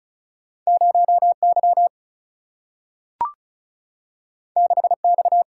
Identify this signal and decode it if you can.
Morse code, heard as a keyed tone, about 36 wpm, 700 Hz. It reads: 0 Y 6 X
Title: Single-Letter Single-Number 35wpm